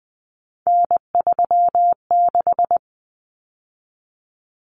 Morse code, heard as a keyed tone, about 20 wpm, 700 Hz. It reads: N36
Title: Common Cut Numbers - 3-Digit 20wpm